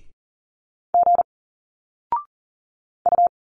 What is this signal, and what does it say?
Morse code, heard as a keyed tone, about 40 wpm, 700 Hz. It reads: G U
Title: Single Letters 40wpm